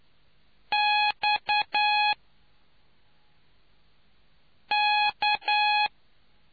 Morse code, heard as a keyed tone, about 10 wpm, 840 Hz. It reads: X K